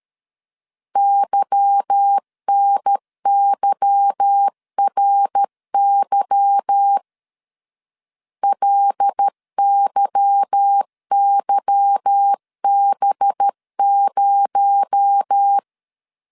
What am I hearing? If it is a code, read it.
Morse code, heard as a keyed tone, about 13 wpm, 800 Hz. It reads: YNYRY LYYB0
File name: LEZIONI / 8 YJ45 / 8 Y 12wpm 2228